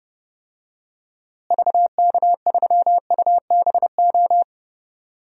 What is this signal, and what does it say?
Morse code, heard as a keyed tone, about 32 wpm, 700 Hz. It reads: VK3UBO
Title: DXCC-100 Call Signs - 6 Characters - Repeated 2x - 30wpm